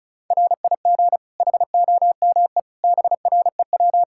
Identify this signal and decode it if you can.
Morse code, heard as a keyed tone, about 36 wpm, 700 Hz. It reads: RIG HOME BREW
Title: K1USN SST QSO Practice 35wpm